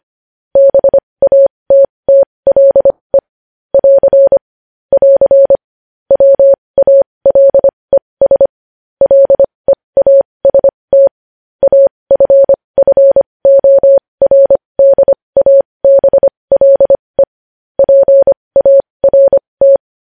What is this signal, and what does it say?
Morse code, heard as a keyed tone, about 25 wpm, spaced as for 20 wpm, 550 Hz. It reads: BATTLE <AR> <AR> WALES LEAST AFFORDABLE PART
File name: podcast / News headlines-1x-25x20